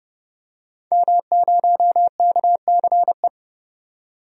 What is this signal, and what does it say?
Morse code, heard as a keyed tone, about 30 wpm, 700 Hz. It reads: M0KCE
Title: DXCC-100 Call Signs - 5 Characters - 30wpm